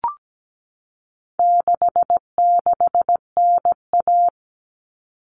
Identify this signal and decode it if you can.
Morse code, heard as a keyed tone, about 17 wpm, 700 Hz. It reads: 66NA
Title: All Cut Numbers - 4-Digit 17wpm